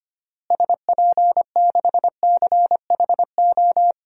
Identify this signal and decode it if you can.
Morse code, heard as a keyed tone, about 25 wpm, 700 Hz. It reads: SP6CHO
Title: DXCC-100 Call Signs - 6 Characters - Repeated 3x - 25wpm